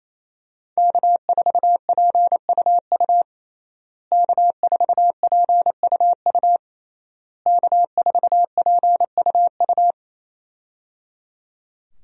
Morse code, heard as a keyed tone, about 28 wpm, 700 Hz. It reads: K4PUU K4PUU K4PUU
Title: USA Call Signs - 5 Characters - Repeated 3x - 28wpm